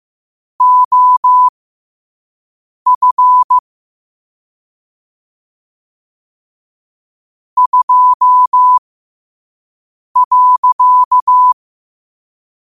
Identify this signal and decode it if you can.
Morse code, heard as a keyed tone, about 15 wpm, spaced as for 5 wpm, 1000 Hz. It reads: OF 2.
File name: qotdfiles / 5wpm